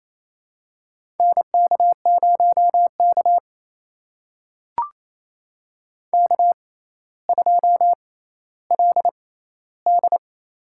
Morse code, heard as a keyed tone, about 28 wpm, spaced as for 14 wpm, 700 Hz. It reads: NK0K K2LD
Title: USA Call Signs - 6x Character Spacing - 28wpm